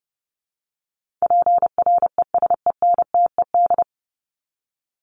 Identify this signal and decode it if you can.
Morse code, heard as a keyed tone, about 30 wpm, 700 Hz. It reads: PRESENTED
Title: ED Suffix - Top 100 Words with Base Words 30wpm